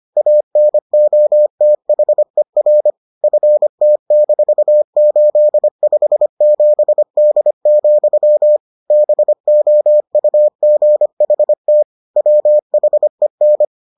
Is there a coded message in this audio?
Yes: Morse code from 0.2 to 13.7 s, heard as a keyed tone, about 25 wpm, 600 Hz. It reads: ANOTHER FT-857D, BOUGHT WHEN